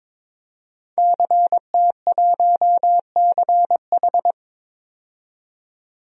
Morse code, heard as a keyed tone, about 22 wpm, 700 Hz. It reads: CT1CH
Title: DXCC-100 Call Signs - 5 Characters - 22wpm